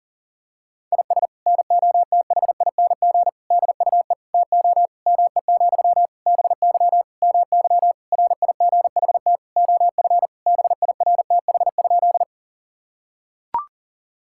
Morse code, heard as a keyed tone, about 37 wpm, 700 Hz. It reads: IS NOTHING DUE TO ME, BY MY RIGHT OF BIRTH?
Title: Sentences from Top 1000 Words - Review Part 6 - 40wpm